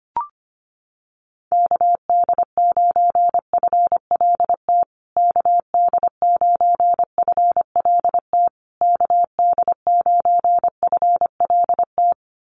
Morse code, heard as a keyed tone, about 25 wpm, 700 Hz. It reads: KD9FLT KD9FLT KD9FLT
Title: USA Call Signs Repeated 3x 25wpm